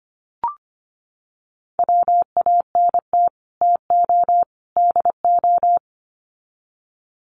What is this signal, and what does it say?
Morse code, heard as a keyed tone, about 25 wpm, 700 Hz. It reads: WANT TO DO
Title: Sets of 3 Words 25wpm